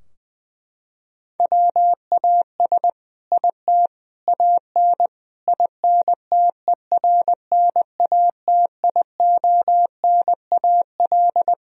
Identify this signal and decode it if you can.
Morse code, heard as a keyed tone, about 20 wpm, 700 Hz. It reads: WAS IT AN INTERNATIONAL
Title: Sentences from Top 1000 Words - Review Part 4 - 20wpm